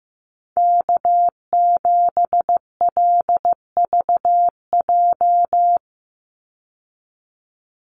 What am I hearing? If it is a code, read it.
Morse code, heard as a keyed tone, about 15 wpm, 700 Hz. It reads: K7LVJ
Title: Call Signs Repeated 3x - 15wpm